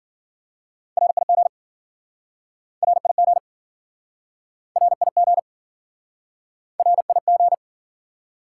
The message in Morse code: RIG RIG RIG RIG

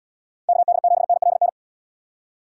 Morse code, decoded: KD7NLN

